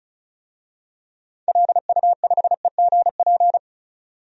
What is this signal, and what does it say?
Morse code, heard as a keyed tone, about 36 wpm, 700 Hz. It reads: LU5EGP
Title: DXCC-100 Call Signs - 6 Characters - 35wpm